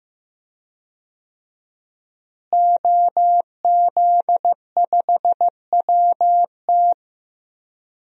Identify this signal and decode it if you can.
Morse code, heard as a keyed tone, about 15 wpm, 700 Hz. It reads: OZ5WT
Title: DXCC-100 Call Signs - 5 Characters - Repeated 3x - 15wpm